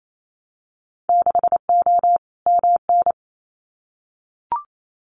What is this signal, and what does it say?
Morse code, heard as a keyed tone, about 28 wpm, 700 Hz. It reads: 6O MN
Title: Winter Field Day Exchanges Encore 28wpm